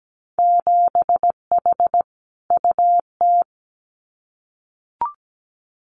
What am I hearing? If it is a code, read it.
Morse code, heard as a keyed tone, about 17 wpm, 700 Hz. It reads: 7H UT
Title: Winter Field Day Exchanges Encore 17wpm